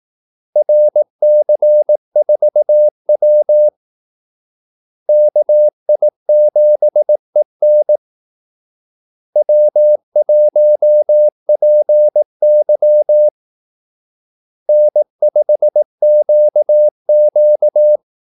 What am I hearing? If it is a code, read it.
Morse code, heard as a keyed tone, about 18 wpm, 600 Hz. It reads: RC4W KI7EN W1PY N5QQ
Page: Kaldesignaler 18 wpm